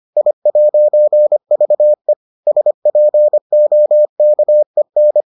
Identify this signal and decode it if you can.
Morse code, heard as a keyed tone, about 25 wpm, 600 Hz. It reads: I'VE SPOKEN